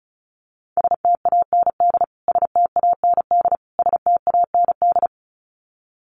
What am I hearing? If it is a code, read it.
Morse code, heard as a keyed tone, about 34 wpm, 700 Hz. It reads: STAND STAND STAND
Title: Top 500 Words Repeated 35wpm